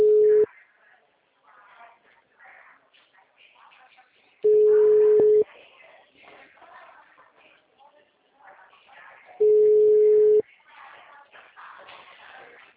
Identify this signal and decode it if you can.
Ringback tone (Europe, 425 Hz), with speech.
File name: out- -2022- - - .81658